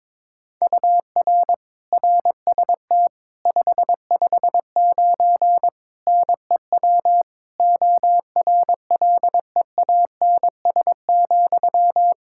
Morse code, heard as a keyed tone, about 22 wpm, 700 Hz. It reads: UR RST 559 NEW ORLEANS,
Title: Practice QSO - Exchange 1 - Beginning - 22wpm